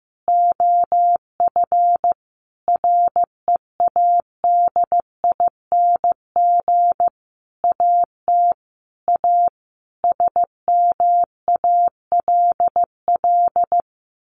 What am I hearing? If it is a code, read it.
Morse code, heard as a keyed tone, about 15 wpm, 700 Hz. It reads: OF READING AT A SMALL
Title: Sentences from Top 900 Words 15wpm